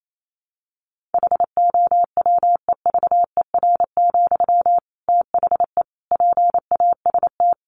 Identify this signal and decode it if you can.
Morse code, heard as a keyed tone, about 28 wpm, 700 Hz. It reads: HOWEVER, THE PAST
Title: Sentences from Top 1400 Words 28wpm